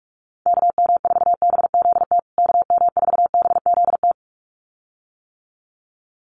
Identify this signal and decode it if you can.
Morse code, heard as a keyed tone, about 44 wpm, 700 Hz. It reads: KM4BZT KM4BZT
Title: USA Call Signs Repeated 2x 45wpm